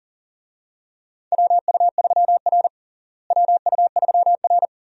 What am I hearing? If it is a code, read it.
Morse code, heard as a keyed tone, about 40 wpm, 700 Hz. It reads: WU3R WU3R
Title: USA Call Signs - 4 Characters - Repeated 2x - 40wpm